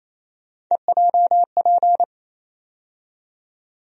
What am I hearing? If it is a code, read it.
Morse code, heard as a keyed tone, about 28 wpm, 700 Hz. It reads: EJP
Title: USA Call Sign Postfixes - 3 Letters 28wpm